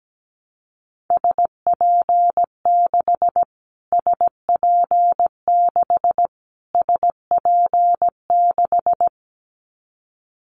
Morse code, heard as a keyed tone, about 17 wpm, 700 Hz. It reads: SP6 SP6 SP6
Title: Callsign Prefixes for Top 25 DXCC - Repeated 3x - 17wpm